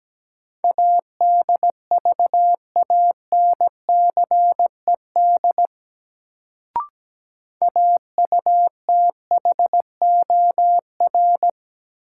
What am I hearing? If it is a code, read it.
Morse code, heard as a keyed tone, about 17 wpm, 700 Hz. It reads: ADVANCED AUTHOR
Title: Sentences from Top 1200 Words - New Word Drill - 17wpm